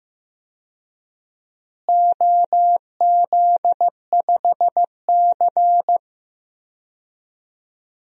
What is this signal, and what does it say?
Morse code, heard as a keyed tone, about 16 wpm, 700 Hz. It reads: OZ5C
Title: DXCC-100 Call Signs - 4 Characters - Repeated 2x - 15wpm